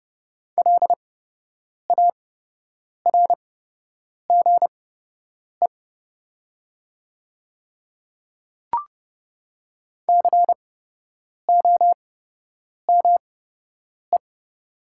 Morse code, heard as a keyed tone, about 30 wpm, spaced as for 8 wpm, 700 Hz. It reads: LARGE COME
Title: 8x Character Spacing - Top 200 Words - 30wpm